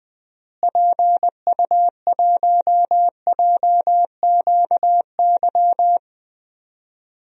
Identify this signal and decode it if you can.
Morse code, heard as a keyed tone, about 21 wpm, 700 Hz. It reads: PU1JQY
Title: DXCC-100 Call Signs - 6 Characters - Repeated 3x - 20wpm